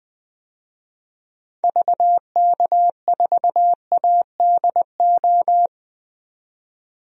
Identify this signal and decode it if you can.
Morse code, heard as a keyed tone, about 20 wpm, 700 Hz. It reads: VK4ADO
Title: DXCC-100 Call Signs - 6 Characters - 20wpm